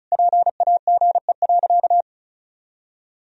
Morse code, heard as a keyed tone, about 36 wpm, 700 Hz. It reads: PAGE.
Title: Sentences from Top 1000 Words - Review Part 1 - 35wpm